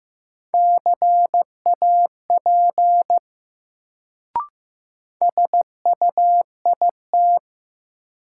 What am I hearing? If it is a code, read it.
Morse code, heard as a keyed tone, about 15 wpm, 700 Hz. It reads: CAP SUIT
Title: Sentences from Top 1600 Words - New Word Drill - 15wpm